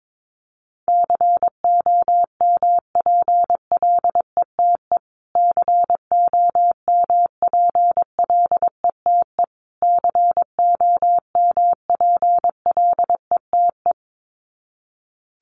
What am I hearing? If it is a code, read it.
Morse code, heard as a keyed tone, about 22 wpm, 700 Hz. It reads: COMPLETE COMPLETE COMPLETE
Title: Top 500 Words Repeated 22wpm